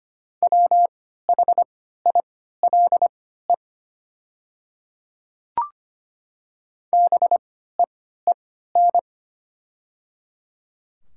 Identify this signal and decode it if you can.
Morse code, heard as a keyed tone, about 25 wpm, spaced as for 14 wpm, 690 Hz. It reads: WHILE BEEN
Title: 3x Character Spacing - Top 500 Words - 25wpm